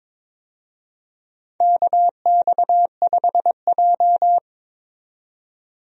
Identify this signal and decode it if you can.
Morse code, heard as a keyed tone, about 22 wpm, 700 Hz. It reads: KX5J
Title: USA Call Signs - 4 Characters - 22wpm